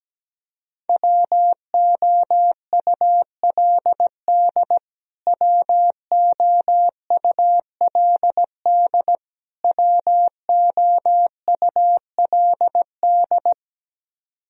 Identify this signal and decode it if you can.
Morse code, heard as a keyed tone, about 17 wpm, 700 Hz. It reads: WOULD WOULD WOULD
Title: Top 100 Words Repeated 17wpm